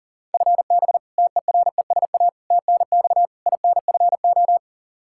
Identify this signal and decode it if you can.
Morse code, heard as a keyed tone, about 40 wpm, 700 Hz. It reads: FB TERESA TNX INFO
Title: Practice QSO - Exchange 3 - More Info - 40wpm